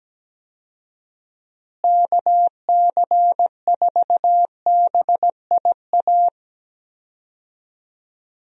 Morse code, heard as a keyed tone, about 17 wpm, 700 Hz. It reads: KC4BIA
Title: USA Call Signs - 6 Characters - Repeated 2x - 17wpm